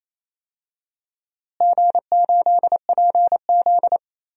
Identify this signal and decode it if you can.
Morse code, heard as a keyed tone, about 28 wpm, 700 Hz. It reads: G8PZ
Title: DXCC-100 Call Signs - 4 Characters - Repeated 2x - 28wpm